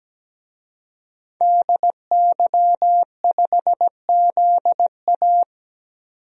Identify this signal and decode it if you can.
Morse code, heard as a keyed tone, about 17 wpm, 700 Hz. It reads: DY5ZA
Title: DXCC-100 Call Signs - 5 Characters - 17wpm